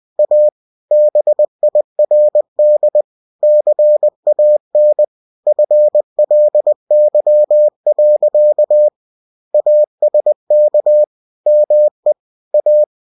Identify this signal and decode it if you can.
Morse code, heard as a keyed tone, about 20 wpm, 600 Hz. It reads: A BIRD CAN FLY. ASK ME A
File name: cw-books / winnie-the-pooh / winnie-20wpm-page-0045